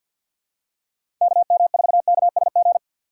Morse code, heard as a keyed tone, about 53 wpm, 700 Hz. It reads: KM4KSG
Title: USA Call Signs - 6 Characters - 6x Character Spacing - 50wpm